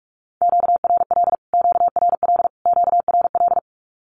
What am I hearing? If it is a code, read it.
Morse code, heard as a keyed tone, about 44 wpm, 700 Hz. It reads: QRL QRL QRL
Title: Q-Codes Repeated 3x 45wpm